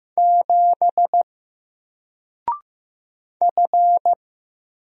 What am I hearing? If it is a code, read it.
Morse code, heard as a keyed tone, about 15 wpm, 700 Hz. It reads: 7 F
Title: Single Character (Letter:Number:Punctuation:Prosign) Alt 15wpm